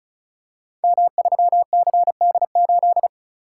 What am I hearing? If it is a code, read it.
Morse code, heard as a keyed tone, about 36 wpm, 700 Hz. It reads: M3CD8
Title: Code Groups - 5 Letter:Numbers - 35wpm